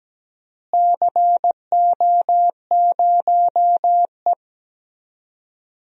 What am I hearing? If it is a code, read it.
Morse code, heard as a keyed tone, about 17 wpm, 700 Hz. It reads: CO0E
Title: DXCC-100 Call Signs - 4 Characters - Repeated 3x - 17wpm